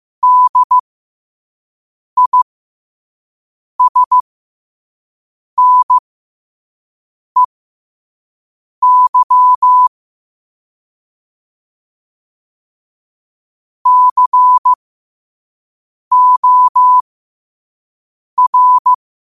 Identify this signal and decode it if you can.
Morse code, heard as a keyed tone, about 15 wpm, spaced as for 5 wpm, 1000 Hz. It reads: DISNEY COR
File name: 5wpm